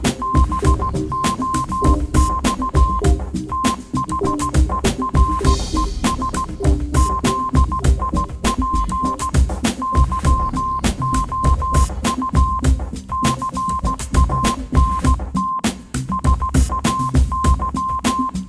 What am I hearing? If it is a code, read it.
Morse code, heard as a keyed tone, about 16 wpm, 1050 Hz. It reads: BOX THERES ZIP YOU CANT SAY
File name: Kirkoid / Kirkoid - Small Stuff (Morse Mix)